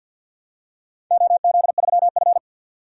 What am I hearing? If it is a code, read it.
Morse code, heard as a keyed tone, about 51 wpm, 700 Hz. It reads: OZ3F
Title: DXCC-100 Call Signs - 4 Characters - 50wpm